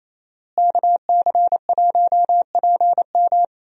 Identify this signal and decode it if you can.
Morse code, heard as a keyed tone, about 27 wpm, 700 Hz. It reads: KC1PM
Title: USA Call Signs - 5 Characters - Repeated 2x - 28wpm